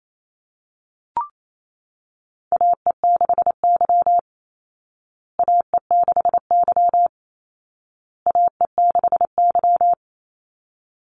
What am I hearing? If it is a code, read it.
Morse code, heard as a keyed tone, about 28 wpm, 700 Hz. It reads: AE6Y AE6Y AE6Y
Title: Call Signs Repeated 2x - 28wpm